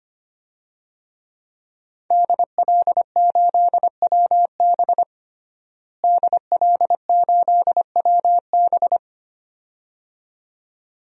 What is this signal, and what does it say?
Morse code, heard as a keyed tone, about 25 wpm, 700 Hz. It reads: DL8WB DL8WB
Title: DXCC-100 Call Signs - 5 Characters - Repeated 2x - 25wpm